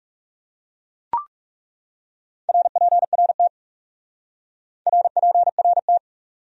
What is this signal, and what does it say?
Morse code, heard as a keyed tone, about 37 wpm, 700 Hz. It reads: RPRT RPRT
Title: Speed-Racing Abbreviations - 30wpm